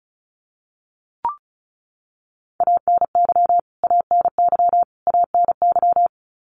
Morse code, heard as a keyed tone, about 34 wpm, 700 Hz. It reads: ANY ANY ANY
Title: Top 500 Words Repeated 35wpm